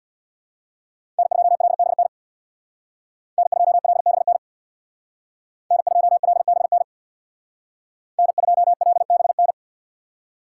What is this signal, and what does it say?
Morse code, heard as a keyed tone, about 64 wpm, 700 Hz. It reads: N2LBN N2LBN N2LBN N2LBN